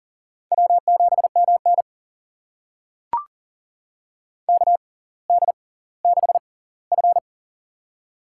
Morse code, heard as a keyed tone, about 40 wpm, spaced as for 20 wpm, 700 Hz. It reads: W7MN KD6F
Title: USA Call Signs - 4 Characters - 6x Character Spacing - 40wpm